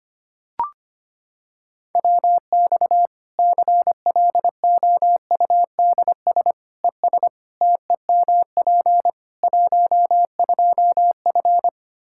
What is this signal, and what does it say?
Morse code, heard as a keyed tone, about 25 wpm, 700 Hz. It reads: WX CLOUDS ES TEMP 12F
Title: Practice QSO - Exchange 2 - Info - 25wpm